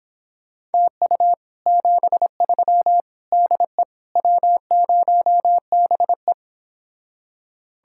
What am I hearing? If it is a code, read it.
Morse code, heard as a keyed tone, about 31 wpm, 700 Hz. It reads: TU 73 DE W0BE